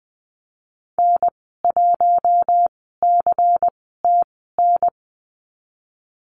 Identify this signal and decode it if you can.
Morse code, heard as a keyed tone, about 20 wpm, spaced as for 14 wpm, 700 Hz. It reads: N1CTN
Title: 128 Lesson 21g - Number 6 - Call Signs 20wpm